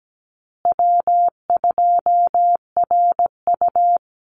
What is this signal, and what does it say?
Morse code, heard as a keyed tone, about 17 wpm, 700 Hz. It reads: W2RU
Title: Call Signs Repeated 3x - 17wpm